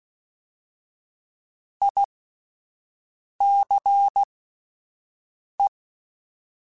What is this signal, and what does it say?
Morse code, heard as a keyed tone, about 16 wpm, spaced as for 5 wpm, 800 Hz. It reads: ICE
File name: General / General-mp3s / Houses @10